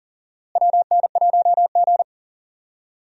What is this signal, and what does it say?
Morse code, heard as a keyed tone, about 38 wpm, 700 Hz. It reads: WN1G
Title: USA Call Signs - 4 Characters - Repeated 3x - 40wpm